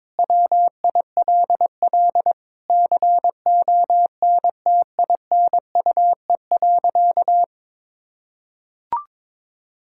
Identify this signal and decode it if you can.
Morse code, heard as a keyed tone, about 22 wpm, 700 Hz. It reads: WILL CONTINUE.
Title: Sentences from Top 1000 Words - Review Part 6 - 22wpm